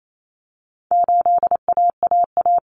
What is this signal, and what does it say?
Morse code, heard as a keyed tone, about 28 wpm, 700 Hz. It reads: 8AAA
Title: Common Cut Numbers - 4-Digit 28wpm